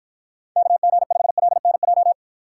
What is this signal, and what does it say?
Morse code, heard as a keyed tone, about 52 wpm, 700 Hz. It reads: KG5LNJ